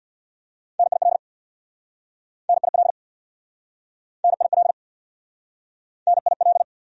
Morse code, heard as a keyed tone, about 69 wpm, 700 Hz. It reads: NIL NIL NIL NIL